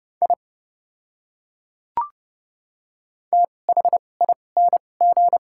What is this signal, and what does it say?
Morse code, heard as a keyed tone, about 32 wpm, spaced as for 22 wpm, 700 Hz. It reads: I THING
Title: 2x Character Spacing - Top 100 Words - 30wpm